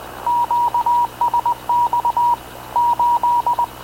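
Morse code, heard as a keyed tone, about 19 wpm, 960 Hz. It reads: QSX 8
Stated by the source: Havana Radio / CLA sending Spanish press (PX) in Morse, recorded by Radio Officer Paul Dumesnil onboard CNAV QUEST/CZDO.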